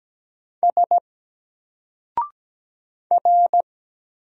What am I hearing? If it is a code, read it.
Morse code, heard as a keyed tone, about 17 wpm, 700 Hz. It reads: S R
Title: Single Character (Letter:Number:Punctuation:Prosign) 17wpm